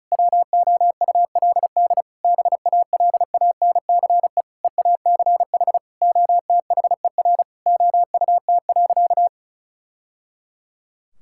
Morse code, heard as a keyed tone, about 36 wpm, 700 Hz. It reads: WOULD BALANCE EACH OTHER OUT.
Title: Sentences from Top 1000 Words - Review Part 6 - 35wpm